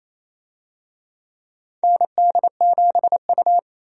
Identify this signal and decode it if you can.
Morse code, heard as a keyed tone, about 28 wpm, 700 Hz. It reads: ND7U
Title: USA Call Signs - 4 Characters - Repeated 3x - 28wpm